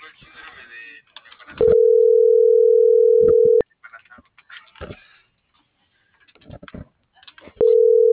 Ringback tone (North America), 440 + 480 Hz.